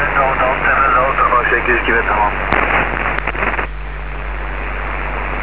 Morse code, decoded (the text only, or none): none